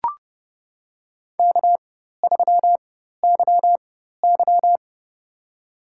Morse code, heard as a keyed tone, about 30 wpm, spaced as for 14 wpm, 700 Hz. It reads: K3YY
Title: USA Call Signs - 4 Characters - 4x Character Spacing - 30wpm